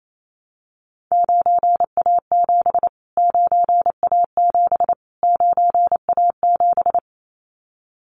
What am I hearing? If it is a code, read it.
Morse code, heard as a keyed tone, about 28 wpm, 700 Hz. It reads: 9A7 9A7 9A7
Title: Callsign Prefixes for Top 25 DXCC - Repeated 3x - 28wpm